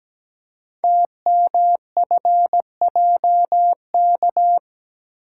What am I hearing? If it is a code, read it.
Morse code, heard as a keyed tone, about 17 wpm, 700 Hz. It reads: TMFJK
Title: Code Groups - 5 Letter:Numbers - 17wpm